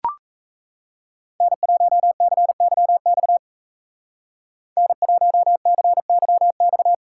Morse code, heard as a keyed tone, about 34 wpm, 700 Hz. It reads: N1CYX N1CYX
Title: Speed Racing - USA 1x3 Call Signs 28wpm